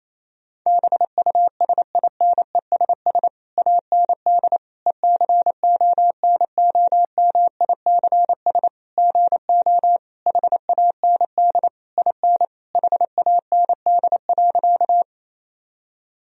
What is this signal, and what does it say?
Morse code, heard as a keyed tone, about 28 wpm, 700 Hz. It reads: BUSINESS AND ECONOMICS GO HAND IN HAND.
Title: Sentences from Top 1000 Words - Review Part 3 - 28wpm